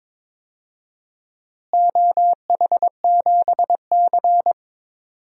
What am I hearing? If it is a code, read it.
Morse code, heard as a keyed tone, about 21 wpm, 700 Hz. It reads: OH7C
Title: DXCC-100 Call Signs - 4 Characters - Repeated 2x - 22wpm